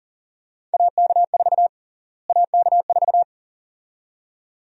Morse code, heard as a keyed tone, about 40 wpm, 700 Hz. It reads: AK4 AK4
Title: USA Call Sign Prefixes - Repeated 2x - 40wpm